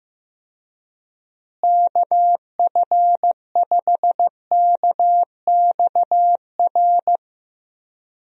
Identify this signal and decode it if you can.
Morse code, heard as a keyed tone, about 15 wpm, 700 Hz. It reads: KF5KXR
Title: USA Call Signs - 6 Characters - Repeated 3x - 15wpm